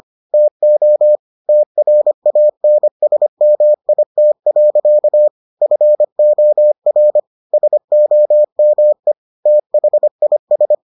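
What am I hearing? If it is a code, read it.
Morse code, heard as a keyed tone, about 25 wpm, 600 Hz. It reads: TO TRANSMIT. FOR SOME THIS
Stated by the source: Morse edition of the Foundations of Amateur Radio episodes.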